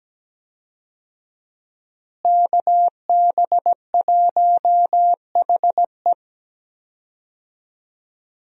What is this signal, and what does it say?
Morse code, heard as a keyed tone, about 17 wpm, 700 Hz. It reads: KB1HE
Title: USA Call Signs - 5 Characters - Repeated 3x - 17wpm